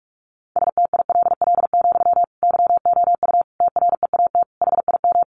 Morse code, heard as a keyed tone, about 44 wpm, 700 Hz. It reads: STILL, YOU TREAT HIM
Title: Sentences from Top 1300 Words 45wpm